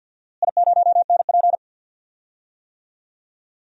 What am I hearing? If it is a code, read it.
Morse code, heard as a keyed tone, about 49 wpm, 700 Hz. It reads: I0NP
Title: DXCC-100 Call Signs - 4 Characters - Repeated 3x - 50wpm